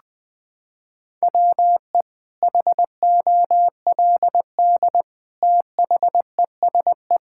Morse code, heard as a keyed tone, about 20 wpm, 700 Hz. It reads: WE HOLD THESE
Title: Sentences with 9-Word Phrases - 20wpm